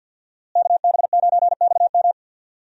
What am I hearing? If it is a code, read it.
Morse code, heard as a keyed tone, about 50 wpm, 700 Hz. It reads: KB9XM
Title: USA Call Signs - 5 Characters - Repeated 3x - 50wpm